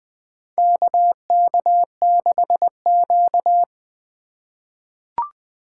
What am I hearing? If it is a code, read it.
Morse code, heard as a keyed tone, about 20 wpm, 700 Hz. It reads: KK6Q
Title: USA Call Signs - 4x Character Spacing - 20wpm